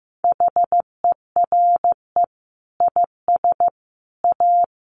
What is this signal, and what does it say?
Morse code, heard as a keyed tone, about 15 wpm, 700 Hz. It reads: HERE IS A
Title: Sentences from Top 1300 Words 15wpm